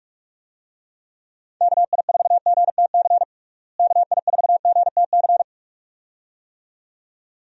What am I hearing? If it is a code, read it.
Morse code, heard as a keyed tone, about 41 wpm, 700 Hz. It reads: KI4GTC KI4GTC
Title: USA Call Signs - 6 Characters - Repeated 2x - 45wpm